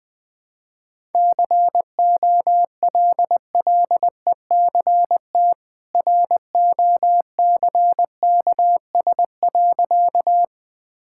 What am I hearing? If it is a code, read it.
Morse code, heard as a keyed tone, about 21 wpm, 700 Hz. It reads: COLLECT ROCKS.
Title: Easy Sentences with Hints - 20wpm